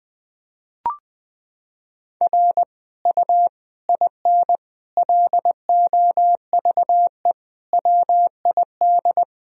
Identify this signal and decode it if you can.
Morse code, heard as a keyed tone, about 20 wpm, 700 Hz. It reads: R U IN LOVE WID
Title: Sentences from Top 400 Words and Abbreviations 20wpm